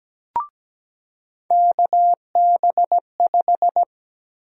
Morse code, heard as a keyed tone, about 17 wpm, 700 Hz. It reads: KB5
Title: USA Call Sign Prefixes - 17wpm